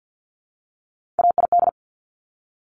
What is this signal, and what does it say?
Morse code, heard as a keyed tone, about 50 wpm, 700 Hz. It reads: AID
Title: Top Three Letter Combinations in Words 50wpm